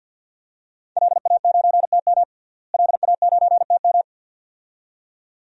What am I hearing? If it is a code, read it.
Morse code, heard as a keyed tone, about 51 wpm, 700 Hz. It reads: LA9TM LA9TM
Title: DXCC-100 Call Signs - 5 Characters - Repeated 2x - 50wpm